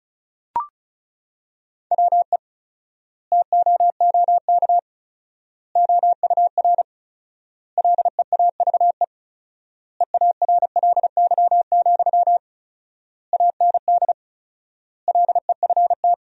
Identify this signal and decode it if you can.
Morse code, heard as a keyed tone, about 34 wpm, 700 Hz. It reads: WE TOOK OUR LEAVE EARLY, AND LEFT
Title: 4x Word-Spacing - Sentences from Top 400 Words - 35wpm